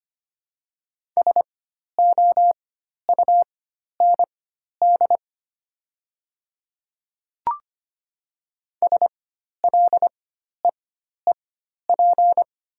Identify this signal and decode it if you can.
Morse code, heard as a keyed tone, about 25 wpm, spaced as for 12 wpm, 700 Hz. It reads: SOUND SLEEP
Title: 4x Character Spacing - Top 500 Words - 25wpm